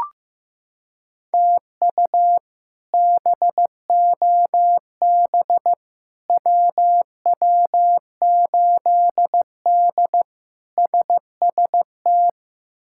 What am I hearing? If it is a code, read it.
Morse code, heard as a keyed tone, about 15 wpm, 700 Hz. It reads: TU BOB WW8D SST
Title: K1USN SST QSO Practice - 15wpm